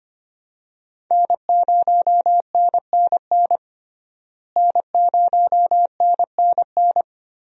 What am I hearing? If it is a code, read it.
Morse code, heard as a keyed tone, about 25 wpm, 700 Hz. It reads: N0NNN N0NNN
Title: USA Call Signs - 5 Characters - Repeated 3x - 25wpm